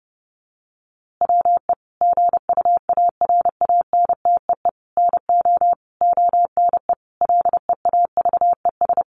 Morse code, heard as a keyed tone, about 30 wpm, 700 Hz. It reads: WE GUARANTEE NO ONE LEAVES
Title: A Sound Of Thunder 30wpm